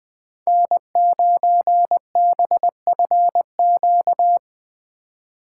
Morse code, heard as a keyed tone, about 20 wpm, 700 Hz. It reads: N9BFQ
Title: Speed Racing - USA 1x3 Call Signs 17wpm